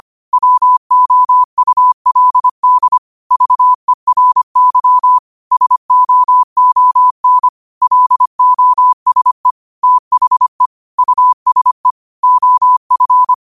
Morse code, heard as a keyed tone, about 25 wpm, 1000 Hz. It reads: WOULD VERY SOON LOSE THE USE OF